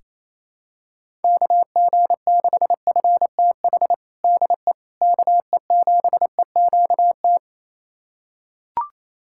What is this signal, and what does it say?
Morse code, heard as a keyed tone, about 28 wpm, 700 Hz. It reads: KG6FTH DE KE7EQT
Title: USA Callsign DE Callsign - 6 Characters - 28wpm